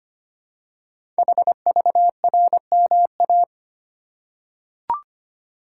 Morse code, heard as a keyed tone, about 25 wpm, 700 Hz. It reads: HVRMA
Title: Code Groups - 5 Letters - 25wpm